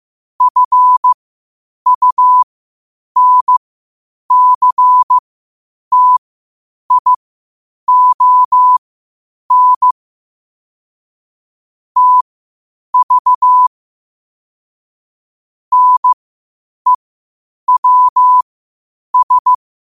Morse code, heard as a keyed tone, about 15 wpm, spaced as for 8 wpm, 1000 Hz. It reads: FUNCTION TV NEWS
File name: qotdfiles / 8wpm